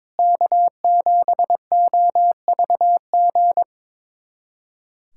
Morse code, heard as a keyed tone, about 21 wpm, 700 Hz. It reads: K7OVG
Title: USA Callsign DE Callsign 22wpm